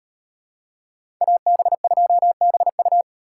Morse code, heard as a keyed tone, about 37 wpm, 700 Hz. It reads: AB2BU